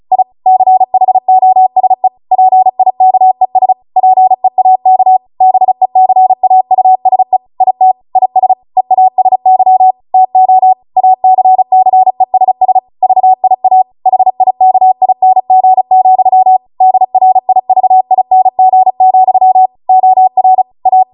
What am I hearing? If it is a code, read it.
Morse code, heard as a keyed tone, about 36 wpm, 750 Hz. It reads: I CHOSE PIKES PEAK BECAUSE IT IS EASY TO ACCESS VIA HIKING, DRIVING, OR A
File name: Morse / 35WPM